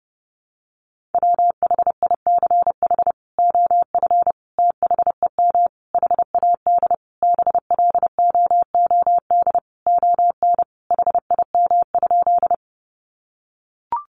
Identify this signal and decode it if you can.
Morse code, heard as a keyed tone, about 30 wpm, 700 Hz. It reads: WHICH OF THEM HAD BLOOD ON HIM?
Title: Sentences from Top 1100 Words 30wpm